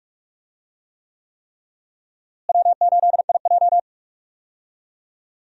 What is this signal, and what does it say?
Morse code, heard as a keyed tone, about 42 wpm, 700 Hz. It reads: W8IJ
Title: USA Call Signs - 4 Characters - Repeated 3x - 45wpm